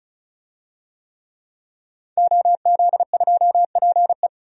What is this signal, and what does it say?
Morse code, heard as a keyed tone, about 36 wpm, 700 Hz. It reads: OZ2PE
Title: DXCC-100 Call Signs - 5 Characters - 35wpm